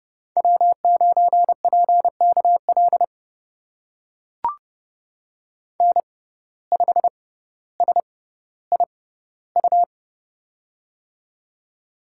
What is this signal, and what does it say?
Morse code, heard as a keyed tone, about 30 wpm, spaced as for 16 wpm, 700 Hz. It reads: W9PKL N5SIU
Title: USA Call Signs - 5 Characters - 6x Character Spacing - 30wpm